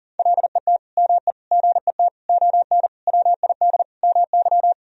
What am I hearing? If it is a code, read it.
Morse code, heard as a keyed tone, about 37 wpm, 700 Hz. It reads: LET ME GET ON WID MY
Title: Sentences from Top 300 Words and Abbreviations 40wpm